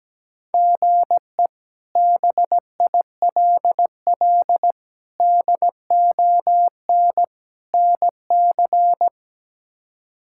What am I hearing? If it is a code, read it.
Morse code, heard as a keyed tone, about 17 wpm, 700 Hz. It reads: GE BILL DON NC
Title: K1USN SST QSO Practice - 17wpm